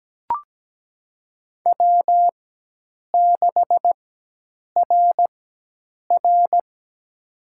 Morse code, heard as a keyed tone, about 17 wpm, spaced as for 8 wpm, 700 Hz. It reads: W6RR
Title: USA Call Signs - 4 Characters - 4x Character Spacing - 17wpm